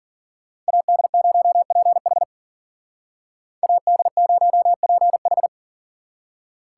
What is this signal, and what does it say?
Morse code, heard as a keyed tone, about 48 wpm, 700 Hz. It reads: AD0PH AD0PH
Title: Speed Racing - USA 2x2 Call Signs 40wpm